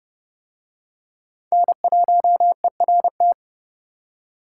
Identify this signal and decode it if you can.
Morse code, heard as a keyed tone, about 30 wpm, 700 Hz. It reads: N1ERT